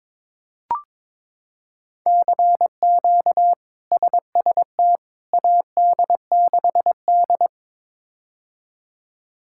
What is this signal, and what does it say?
Morse code, heard as a keyed tone, about 22 wpm, 700 Hz. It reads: CQ SST AD6D
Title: K1USN SST QSO Practice - 22wpm